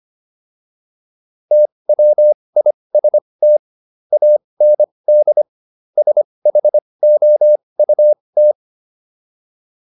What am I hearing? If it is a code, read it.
Morse code, heard as a keyed tone, about 25 wpm, spaced as for 20 wpm, 600 Hz. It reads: TWIST AND SHOUT